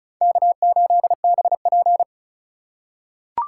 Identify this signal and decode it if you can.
Morse code, heard as a keyed tone, about 36 wpm, 700 Hz. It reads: K8BP
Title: USA Call Signs - 4 Characters - 4x Character Spacing - 35wpm